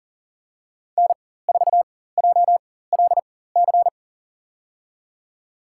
Morse code, heard as a keyed tone, about 37 wpm, spaced as for 18 wpm, 700 Hz. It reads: N4JLC
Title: USA Call Signs - 5 Characters - 4x Character Spacing - 40wpm